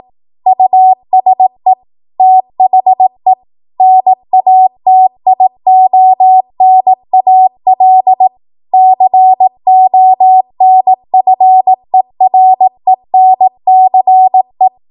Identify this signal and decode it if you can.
Morse code, heard as a keyed tone, about 18 wpm, 750 Hz. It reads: USE THE NATIONAL CONFERENCE